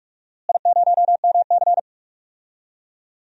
Morse code, heard as a keyed tone, about 42 wpm, 700 Hz. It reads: I0MC
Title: DXCC-100 Call Signs - 4 Characters - Repeated 3x - 45wpm